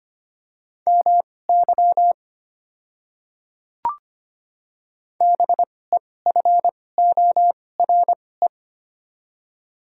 Morse code, heard as a keyed tone, about 25 wpm, spaced as for 18 wpm, 700 Hz. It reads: MY BEFORE
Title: 2x Character Spacing - Top 200 Words - 25wpm